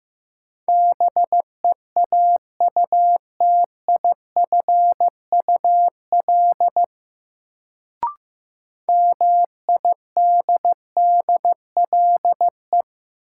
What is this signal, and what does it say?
Morse code, heard as a keyed tone, about 15 wpm, 700 Hz. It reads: BEAUTIFUL MIDDLE
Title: Sentences from Top 900 Words - New Word Drill - 15wpm